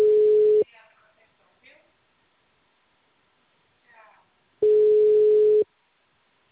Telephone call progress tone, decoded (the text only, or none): ringback tone